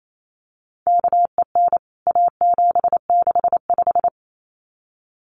Morse code, heard as a keyed tone, about 28 wpm, 700 Hz. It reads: KEN A765
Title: CWT Exchange 28wpm